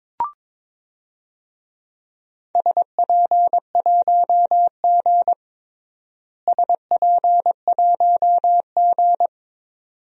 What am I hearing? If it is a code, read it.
Morse code, heard as a keyed tone, about 22 wpm, 700 Hz. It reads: SP1G SP1G
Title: DXCC-100 Call Signs - 4 Characters - Repeated 2x - 22wpm